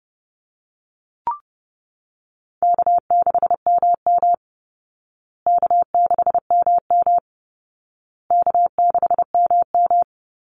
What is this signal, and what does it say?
Morse code, heard as a keyed tone, about 30 wpm, 700 Hz. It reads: K6MM K6MM K6MM
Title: Easy Call Signs Repeated 3x - 30wpm